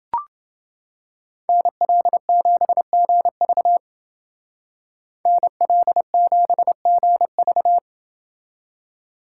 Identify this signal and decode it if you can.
Morse code, heard as a keyed tone, about 29 wpm, 700 Hz. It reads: NL7GV NL7GV